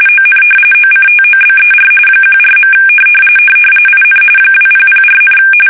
RTTY, about 45.45 baud. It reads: ON OCTOBER 14. IN 2004,
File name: fsk850 / battleshipnj